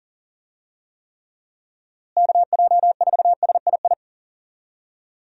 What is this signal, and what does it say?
Morse code, heard as a keyed tone, about 37 wpm, 700 Hz. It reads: KJ4SII
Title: USA Call Signs - 6 Characters - Repeated 2x - 40wpm